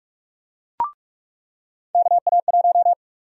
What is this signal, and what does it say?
Morse code, heard as a keyed tone, about 41 wpm, 700 Hz. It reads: KA1
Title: USA Call Sign Prefixes - 45wpm